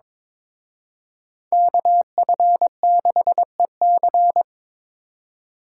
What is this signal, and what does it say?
Morse code, heard as a keyed tone, about 26 wpm, 700 Hz. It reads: KF6EC
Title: Speed Racing - USA Call Signs 22wpm